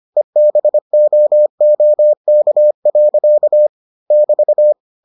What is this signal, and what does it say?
Morse code, heard as a keyed tone, about 25 wpm, 600 Hz. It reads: EBOOK. <BT>